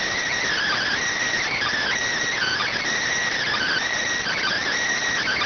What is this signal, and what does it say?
Emergency Alert System (SAME) tones, heard as harsh buzzing bursts. It is heard from 0.5 to 4.6 s.